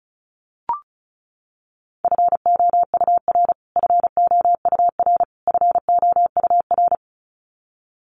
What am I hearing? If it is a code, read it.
Morse code, heard as a keyed tone, about 34 wpm, 700 Hz. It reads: FOUR FOUR FOUR
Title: Top 500 Words Repeated 35wpm